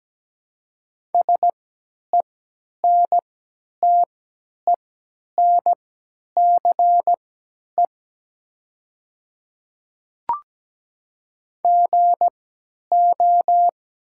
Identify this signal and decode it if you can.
Morse code, heard as a keyed tone, about 17 wpm, spaced as for 10 wpm, 700 Hz. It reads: SENTENCE GO
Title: 3x Character Spacing - Top 200 Words - 17wpm